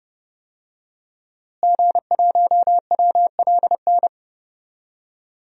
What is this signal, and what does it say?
Morse code, heard as a keyed tone, about 30 wpm, 700 Hz. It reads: G1WLN
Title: DXCC-100 Call Signs - 5 Characters - Repeated 3x - 30wpm